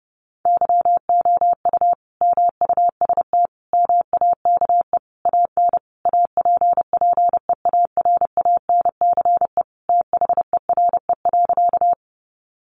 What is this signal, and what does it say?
Morse code, heard as a keyed tone, about 30 wpm, 700 Hz. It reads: YOU MUST MAKE AN APPEARANCE THERE.
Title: Sentences from Top 1100 Words 30wpm